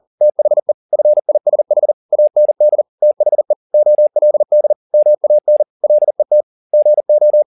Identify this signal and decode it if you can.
Morse code, heard as a keyed tone, about 40 wpm, 600 Hz. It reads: THE FISH AND THE OLD MAN LET GO
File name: the old man and the sea 2 40wpm